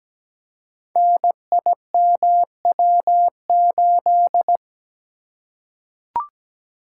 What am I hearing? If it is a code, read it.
Morse code, heard as a keyed tone, about 17 wpm, 700 Hz. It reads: NIMW8
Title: Code Groups - 5 Letter:Numbers - 17wpm